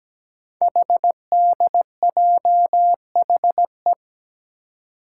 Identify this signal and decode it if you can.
Morse code, heard as a keyed tone, about 17 wpm, 700 Hz. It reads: HDJHE
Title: Code Groups - 5 Letters - 17wpm